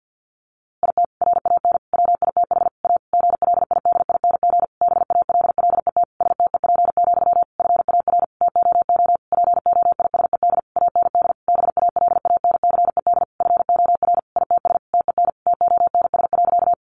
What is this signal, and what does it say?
Morse code, heard as a keyed tone, about 50 wpm, 700 Hz. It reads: IT RAN WITH A GLIDING BALLET STEP, FAR TOO POISED AND BALANCED FOR ITS TEN TONS.
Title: A Sound Of Thunder 50wpm